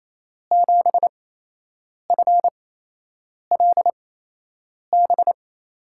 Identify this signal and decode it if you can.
Morse code, heard as a keyed tone, about 28 wpm, spaced as for 8 wpm, 700 Hz. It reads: 7FLB
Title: USA Call Signs - 6 Characters - 8x Character Spacing - 28wpm